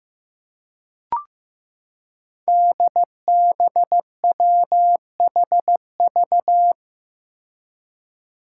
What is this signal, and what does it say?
Morse code, heard as a keyed tone, about 15 wpm, 700 Hz. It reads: DBWHV
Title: Five Challenging Letter-Number Combinations 15wpm